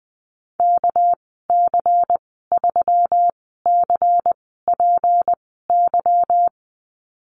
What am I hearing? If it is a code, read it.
Morse code, heard as a keyed tone, about 20 wpm, spaced as for 15 wpm, 700 Hz. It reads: KC3CPY
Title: 207 Lesson 33g - Letter j - Call Signs 20wpm